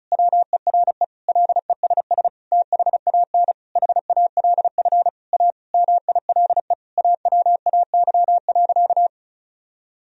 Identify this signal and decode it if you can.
Morse code, heard as a keyed tone, about 36 wpm, 700 Hz. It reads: WERE LESS THAN HALF A MILE AWAY.
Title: Sentences from Top 1000 Words - Review Part 2 - 35wpm